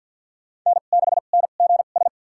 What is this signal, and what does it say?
Morse code, heard as a keyed tone, about 47 wpm, spaced as for 35 wpm, 700 Hz. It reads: N6NGS
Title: USA Call Signs - 5 Characters - 2x Character Spacing - 50wpm